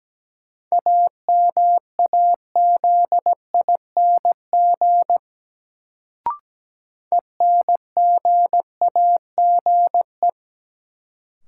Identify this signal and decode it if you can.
Morse code, heard as a keyed tone, about 17 wpm, 700 Hz. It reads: AMAZING ENGAGE
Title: Sentences from Top 1500 Words - New Word Drill - 17wpm